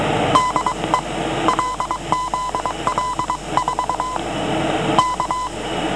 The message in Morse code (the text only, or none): DE LZL4 K